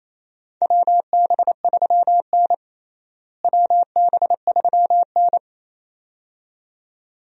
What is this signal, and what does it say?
Morse code, heard as a keyed tone, about 28 wpm, 700 Hz. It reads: WB3N WB3N
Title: USA Call Signs - 4 Characters - Repeated 2x - 28wpm